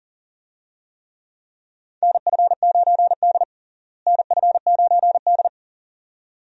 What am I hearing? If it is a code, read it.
Morse code, heard as a keyed tone, about 40 wpm, 700 Hz. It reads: NF9D NF9D
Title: USA Call Signs - 4 Characters - Repeated 2x - 40wpm